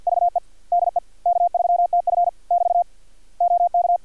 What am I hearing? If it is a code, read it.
Morse code, heard as a keyed tone, about 50 wpm, 700 Hz. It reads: VE DE K3TF <BT> OK
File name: archived / CRSnet